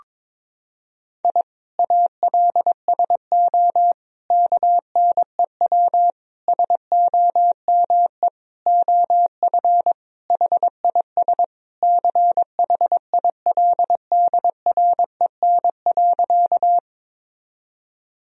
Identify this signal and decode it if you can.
Morse code, heard as a keyed tone, about 21 wpm, 700 Hz. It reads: I ALSO KNEW SOME OF HIS CHILDREN.
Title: Sentences from Top 1000 Words - Review Part 1 - 22wpm